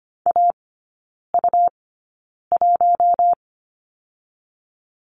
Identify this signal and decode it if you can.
Morse code, heard as a keyed tone, about 25 wpm, spaced as for 9 wpm, 700 Hz. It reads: AU1
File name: Lesson 17e - Letter u - Three Characters 25wpm